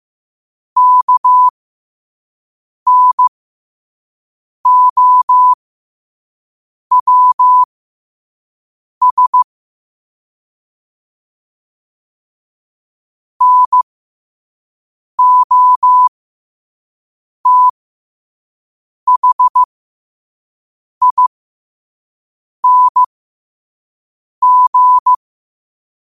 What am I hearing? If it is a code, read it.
Morse code, heard as a keyed tone, about 15 wpm, spaced as for 5 wpm, 1000 Hz. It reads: KNOWS NOTHING
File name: qotdfiles / 5wpm